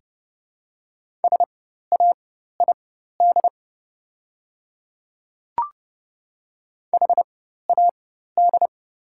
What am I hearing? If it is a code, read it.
Morse code, heard as a keyed tone, about 30 wpm, spaced as for 14 wpm, 700 Hz. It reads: SAID HAD
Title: 4x Character Spacing - Top 100 Words - 30wpm